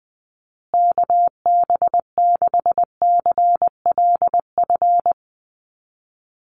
Morse code, heard as a keyed tone, about 20 wpm, 700 Hz. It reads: KB6CLF
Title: USA Call Signs Repeated 3x 20wpm